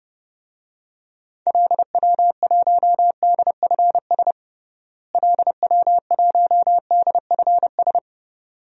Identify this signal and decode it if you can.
Morse code, heard as a keyed tone, about 30 wpm, 700 Hz. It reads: LW1DFS LW1DFS
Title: DXCC-100 Call Signs - 6 Characters - Repeated 2x - 30wpm